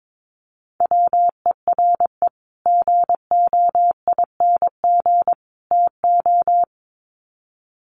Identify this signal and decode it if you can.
Morse code, heard as a keyed tone, about 22 wpm, 700 Hz. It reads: WERE GOING TO
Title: Sets of 3 Words 22wpm